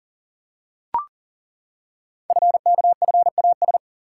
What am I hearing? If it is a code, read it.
Morse code, heard as a keyed tone, about 38 wpm, 690 Hz. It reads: FKFAS
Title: Code Groups - 5 Letter:Numbers - 40wpm